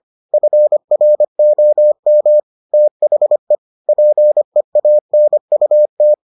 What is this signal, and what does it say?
Morse code, heard as a keyed tone, about 25 wpm, 600 Hz. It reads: FROM THE PEANUT